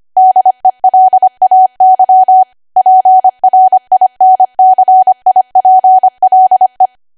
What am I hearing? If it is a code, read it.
Morse code, heard as a keyed tone, about 25 wpm, 750 Hz. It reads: DELAY PRINCIPLE
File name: Archive / 25 WPM / 25WPM